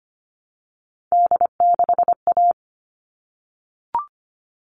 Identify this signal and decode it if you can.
Morse code, heard as a keyed tone, about 25 wpm, 700 Hz. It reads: D6A
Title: All Cut Numbers - 3-Digit 25wpm